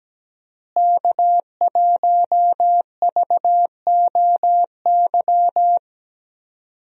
Morse code, heard as a keyed tone, about 17 wpm, 700 Hz. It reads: K1VOY
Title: USA Call Signs - 5 Characters - Repeated 2x - 17wpm